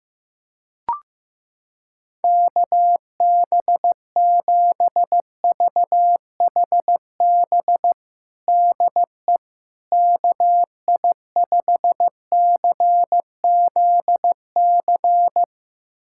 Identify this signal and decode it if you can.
Morse code, heard as a keyed tone, about 15 wpm, 700 Hz. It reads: KB7VHB DE KI5CZC
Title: USA Callsign DE Callsign - 6 Characters - 15wpm